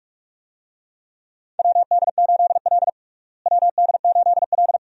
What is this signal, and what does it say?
Morse code, heard as a keyed tone, about 46 wpm, 700 Hz. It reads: WD8L WD8L
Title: USA Call Signs - 4 Characters - Repeated 2x - 45wpm